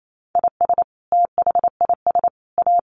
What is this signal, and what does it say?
Morse code, heard as a keyed tone, about 28 wpm, 700 Hz. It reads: IS THIS A
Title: Sentences from Top 900 Words 28wpm